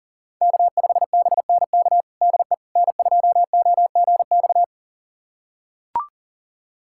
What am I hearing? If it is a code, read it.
Morse code, heard as a keyed tone, about 40 wpm, 700 Hz. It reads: K5BNK DE N2OGX
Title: USA Callsign DE Callsign - 5 Characters - 40wpm